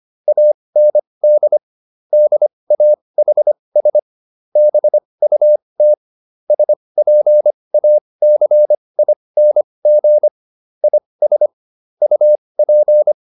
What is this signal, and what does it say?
Morse code, heard as a keyed tone, about 25 wpm, spaced as for 20 wpm, 600 Hz. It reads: AND DAHS BUT SPACING IS UP